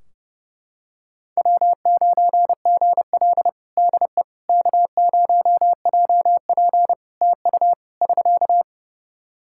Morse code, heard as a keyed tone, about 30 wpm, 700 Hz. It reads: W9GL DE K0JP TU <SK>
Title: Practice QSO - Exchange 4 - Ending - 30wpm